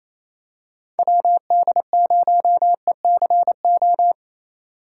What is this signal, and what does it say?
Morse code, heard as a keyed tone, about 28 wpm, 700 Hz. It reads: WD0ECO
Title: K1USN SST QSO Practice - 28wpm